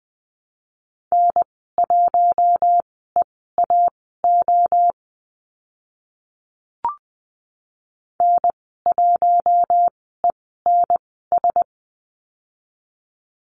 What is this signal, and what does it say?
Morse code, heard as a keyed tone, about 20 wpm, spaced as for 14 wpm, 700 Hz. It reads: N1EAO N1ENS
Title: 069 Lesson 12g - Letter d - Call Signs 20wpm